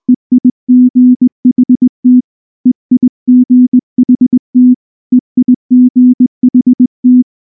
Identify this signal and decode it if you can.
Morse code, heard as a keyed tone, about 20 wpm, 250 Hz. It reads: EIGHT EIGHT EIGHT
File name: morse / morse-training-20WPM-4